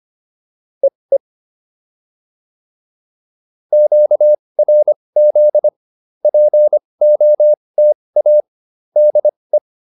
Morse code, heard as a keyed tone, about 25 wpm, spaced as for 20 wpm, 590 Hz. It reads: EE QRZ POTA DE